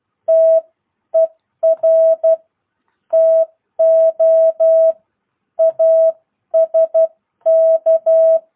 Morse code, heard as a keyed tone, about 12 wpm, spaced as for 10 wpm, 650 Hz. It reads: TER TO ASK